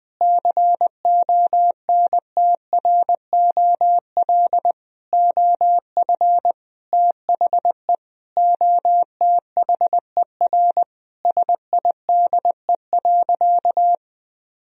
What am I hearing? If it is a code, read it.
Morse code, heard as a keyed tone, about 20 wpm, 700 Hz. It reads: CONTROL OF THE OTHER SIDE.
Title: Sentences from Top 1000 Words - Review Part 3 - 20wpm